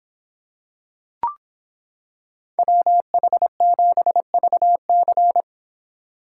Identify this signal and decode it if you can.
Morse code, heard as a keyed tone, about 26 wpm, 700 Hz. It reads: WH7VC